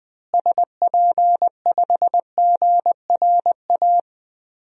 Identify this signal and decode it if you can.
Morse code, heard as a keyed tone, about 20 wpm, 700 Hz. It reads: SP5GRA
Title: DXCC-100 Call Signs - 6 Characters - Repeated 2x - 20wpm